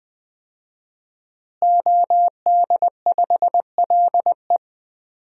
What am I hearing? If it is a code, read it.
Morse code, heard as a keyed tone, about 20 wpm, 700 Hz. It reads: OD5LE
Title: DXCC-100 Call Signs - 5 Characters - Repeated 2x - 20wpm